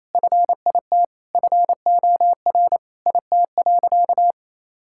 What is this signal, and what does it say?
Morse code, heard as a keyed tone, about 27 wpm, 700 Hz. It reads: FIT FOR IT.
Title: Sentences from Top 1000 Words - Review Part 5 - 28wpm